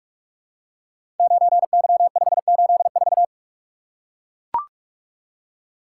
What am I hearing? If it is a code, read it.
Morse code, heard as a keyed tone, about 44 wpm, 700 Hz. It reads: 9Y584
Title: Code Groups - 5 Letter:Numbers - 45wpm